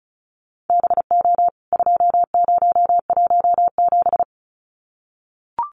Morse code, heard as a keyed tone, about 34 wpm, 700 Hz. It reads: BO 2017
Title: CWT Exchange 35wpm